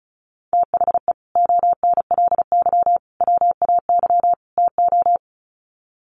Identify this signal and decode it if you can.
Morse code, heard as a keyed tone, about 34 wpm, 700 Hz. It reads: THE ONLY WAY TO
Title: Sets of 4 Words 35wpm